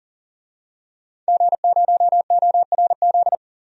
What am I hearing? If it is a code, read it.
Morse code, heard as a keyed tone, about 40 wpm, 700 Hz. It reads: G0ORZ
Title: DXCC-100 Call Signs - 5 Characters - 40wpm